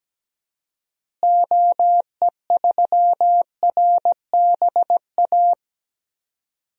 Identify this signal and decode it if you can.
Morse code, heard as a keyed tone, about 17 wpm, 700 Hz. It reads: OE3RBA
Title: DXCC-100 Call Signs - 6 Characters - Repeated 3x - 17wpm